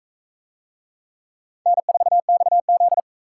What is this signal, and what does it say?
Morse code, heard as a keyed tone, about 38 wpm, 700 Hz. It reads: N4XZ